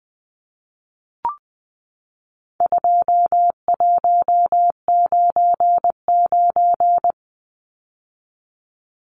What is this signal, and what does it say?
Morse code, heard as a keyed tone, about 20 wpm, 700 Hz. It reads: 2199
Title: Numbers - 4-Digit 20wpm